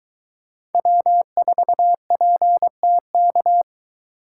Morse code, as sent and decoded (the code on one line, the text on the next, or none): .-- ....- .--. - -.-
W4PTK